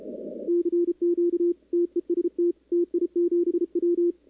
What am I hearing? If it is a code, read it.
Morse code, heard as a keyed tone, about 30 wpm, 340 Hz, with radio noise. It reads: CQ TEST TI7W
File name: contests / 2019-cw-wpx-kt5j-k5pi- -2330 0-11